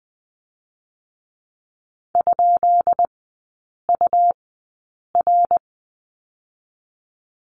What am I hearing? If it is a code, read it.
Morse code, heard as a keyed tone, about 20 wpm, 700 Hz. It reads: ? U R
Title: 145 Lesson 24e - Letter y - Three Characters 20wpm